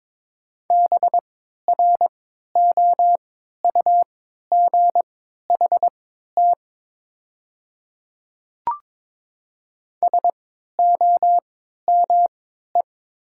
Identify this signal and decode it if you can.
Morse code, heard as a keyed tone, about 22 wpm, spaced as for 12 wpm, 690 Hz. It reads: BROUGHT SOME
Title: 3x Character Spacing - Top 500 Words - 22wpm